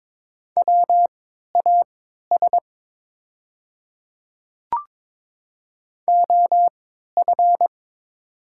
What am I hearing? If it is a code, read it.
Morse code, heard as a keyed tone, about 22 wpm, spaced as for 13 wpm, 700 Hz. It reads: WAS OF
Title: 3x Character Spacing - Top 200 Words - 22wpm